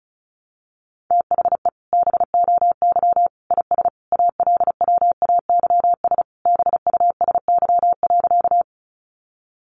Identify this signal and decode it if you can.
Morse code, heard as a keyed tone, about 34 wpm, 700 Hz. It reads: THE BOY IS ALWAYS BUSY.
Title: Additional Easy Phrases 35wpm